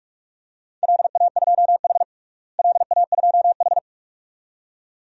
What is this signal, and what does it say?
Morse code, heard as a keyed tone, about 41 wpm, 700 Hz. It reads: LA2H LA2H
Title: DXCC-100 Call Signs - 4 Characters - Repeated 2x - 45wpm